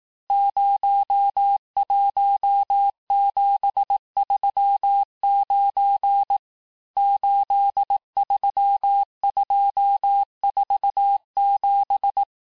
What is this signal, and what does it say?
Morse code, heard as a keyed tone, about 18 wpm, 800 Hz. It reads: 01739 83247
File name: cours-du-lundi files / cours-1594--teste-chiffre-vitesse-17mpm